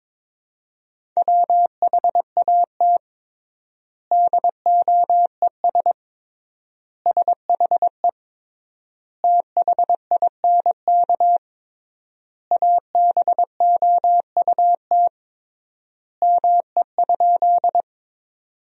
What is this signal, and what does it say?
Morse code, heard as a keyed tone, about 21 wpm, 700 Hz. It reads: WHAT DOES SHE THINK ABOUT ME?
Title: 3x Word-Spacing - Sentences from Top 200 Words - 22wpm